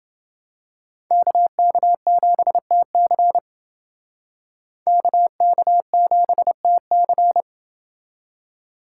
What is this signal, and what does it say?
Morse code, heard as a keyed tone, about 28 wpm, 700 Hz. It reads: KK7TC KK7TC